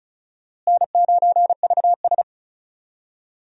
Morse code, heard as a keyed tone, about 36 wpm, 700 Hz. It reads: N9VS
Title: USA Call Signs - 4 Characters - Repeated 2x - 35wpm